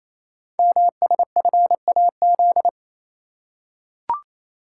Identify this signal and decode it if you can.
Morse code, heard as a keyed tone, about 29 wpm, 700 Hz. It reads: MSFAZ
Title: Code Groups - 5 Letters - 28wpm